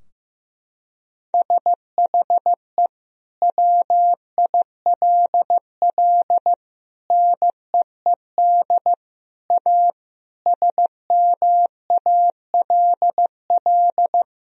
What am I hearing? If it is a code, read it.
Morse code, heard as a keyed tone, about 16 wpm, 700 Hz. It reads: SHE WILL NEED A SMALL
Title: Sentences from Top 1000 Words - Review Part 2 - 15wpm